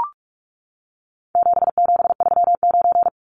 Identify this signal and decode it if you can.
Morse code, heard as a keyed tone, about 44 wpm, 700 Hz. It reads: 7739
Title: Numbers - 4-Digit 45wpm